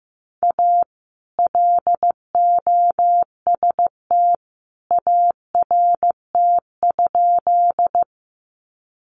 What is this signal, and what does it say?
Morse code, heard as a keyed tone, about 15 wpm, 700 Hz. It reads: A LOST ART?
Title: Sentences from Top 1400 Words 15wpm